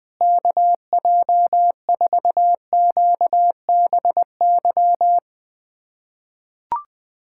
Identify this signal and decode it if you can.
Morse code, heard as a keyed tone, about 21 wpm, 700 Hz. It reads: KJ4QBY
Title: USA Callsign DE Callsign - 6 Characters - 20wpm